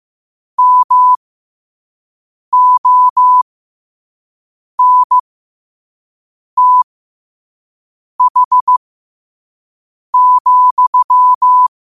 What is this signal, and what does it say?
Morse code, heard as a keyed tone, about 15 wpm, spaced as for 5 wpm, 1000 Hz. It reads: MONTH,
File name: qotdfiles / 5wpm